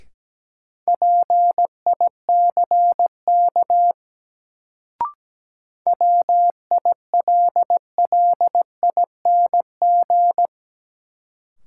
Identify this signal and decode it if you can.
Morse code, heard as a keyed tone, about 17 wpm, 700 Hz. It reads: PICK WILLING
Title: Sentences from Top 1000 Words - New Word Drill - 17wpm